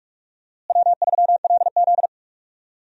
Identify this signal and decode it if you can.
Morse code, heard as a keyed tone, about 46 wpm, 700 Hz. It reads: W3LZ
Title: USA Call Signs - 4 Characters - Repeated 3x - 45wpm